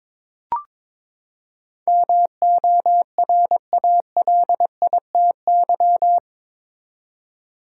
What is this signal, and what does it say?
Morse code, heard as a keyed tone, about 22 wpm, 700 Hz. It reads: MORALITY
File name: practice / 22wpm / Eight Letter Words - 22wpm